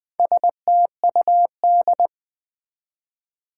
Morse code, heard as a keyed tone, about 20 wpm, 700 Hz. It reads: STUD
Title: Build A Word - 2 Letters At A Time - 6-Letter Words - 20wpm